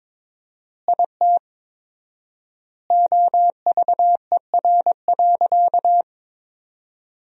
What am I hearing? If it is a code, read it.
Morse code, heard as a keyed tone, about 21 wpm, 700 Hz. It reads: IT OVER.
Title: 4x Word-Spacing - Sentences from Top 500 Words - 22wpm